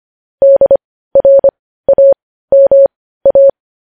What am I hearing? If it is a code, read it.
Morse code, heard as a keyed tone, about 25 wpm, spaced as for 15 wpm, 550 Hz. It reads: DRAMA
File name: podcast / News headlines-1x-25x15